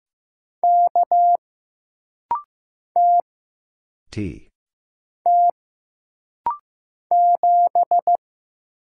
Morse code, heard as a keyed tone, about 15 wpm, 700 Hz. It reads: K T T 7
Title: Single Letter-Number Alt - Rapid-Fire 15wpm